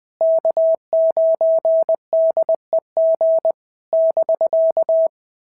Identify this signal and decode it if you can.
Morse code, heard as a keyed tone, about 20 wpm, 650 Hz. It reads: K9DEG <BK>
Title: POTA QSO Practice - 20wpm